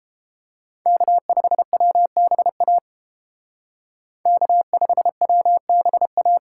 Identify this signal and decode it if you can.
Morse code, heard as a keyed tone, about 29 wpm, 700 Hz. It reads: K5WBA K5WBA